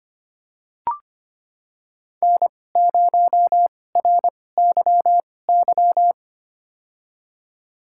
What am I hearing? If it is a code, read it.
Morse code, heard as a keyed tone, about 25 wpm, spaced as for 18 wpm, 700 Hz. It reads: N0RYY
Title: USA Call Signs - 5 Characters - 2x Character Spacing - 25wpm